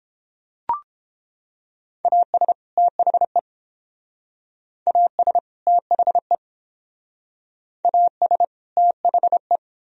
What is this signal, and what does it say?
Morse code, heard as a keyed tone, about 29 wpm, 700 Hz. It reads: AS THE AS THE AS THE